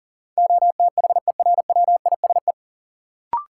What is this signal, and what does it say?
Morse code, heard as a keyed tone, about 39 wpm, 700 Hz. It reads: OTHERWISE
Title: Sentences from Top 1200 Words - New Word Drill - 40wpm